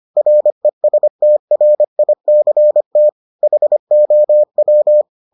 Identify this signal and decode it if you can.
Morse code, heard as a keyed tone, about 25 wpm, 600 Hz. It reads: RESTRICT HOW